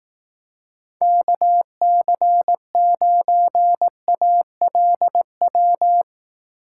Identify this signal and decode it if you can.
Morse code, heard as a keyed tone, about 18 wpm, 700 Hz. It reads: KC9ALW
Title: Speed Racing - USA Call Signs 15wpm